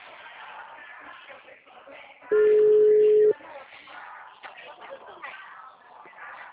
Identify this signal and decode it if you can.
Ringback tone (Europe, 425 Hz), with speech.